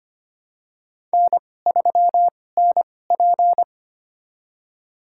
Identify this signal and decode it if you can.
Morse code, heard as a keyed tone, about 25 wpm, spaced as for 18 wpm, 700 Hz. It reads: N3NP
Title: USA Call Signs - 4 Characters - 2x Character Spacing - 25wpm